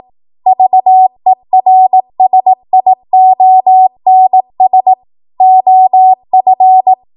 Morse code, heard as a keyed tone, about 18 wpm, 750 Hz. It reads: VERSIONS OF
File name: Morse / 18WPM